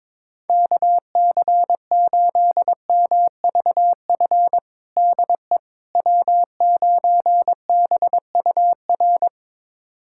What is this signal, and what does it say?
Morse code, heard as a keyed tone, about 21 wpm, 700 Hz. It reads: KC8MVF DE W9BUR
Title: USA Callsign DE Callsign 22wpm